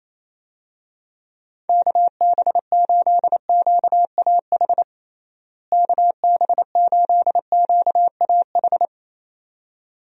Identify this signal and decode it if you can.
Morse code, heard as a keyed tone, about 28 wpm, 700 Hz. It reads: KB8QAH KB8QAH
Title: USA Call Signs - 6 Characters - Repeated 2x - 28wpm